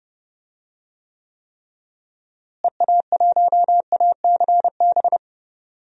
Morse code, heard as a keyed tone, about 30 wpm, 700 Hz. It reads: EA1ACB
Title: DXCC-100 Call Signs - 6 Characters - Repeated 3x - 30wpm